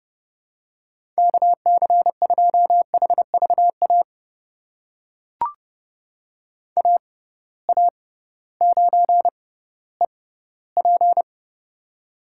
Morse code, heard as a keyed tone, about 30 wpm, spaced as for 16 wpm, 700 Hz. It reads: KC2HVA AA9EP
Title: USA Call Signs - 6x Character Spacing - 30wpm